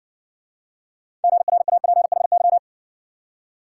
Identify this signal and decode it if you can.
Morse code, heard as a keyed tone, about 60 wpm, 700 Hz. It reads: GRAPHY